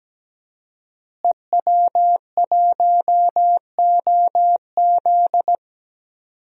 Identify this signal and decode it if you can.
Morse code, heard as a keyed tone, about 17 wpm, 700 Hz. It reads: EW1OZ
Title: DXCC-100 Call Signs - 5 Characters - Repeated 3x - 17wpm